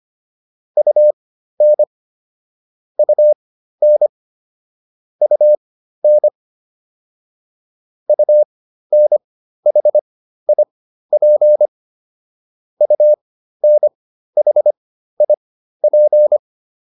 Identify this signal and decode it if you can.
Morse code, heard as a keyed tone, about 25 wpm, spaced as for 13 wpm, 600 Hz. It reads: UN UN UN UNHIP UNHIP
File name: UN5 13